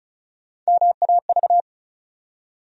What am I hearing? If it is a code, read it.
Morse code, heard as a keyed tone, about 36 wpm, 700 Hz. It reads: MAV
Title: USA Call Sign Postfixes - 3 Letters 35wpm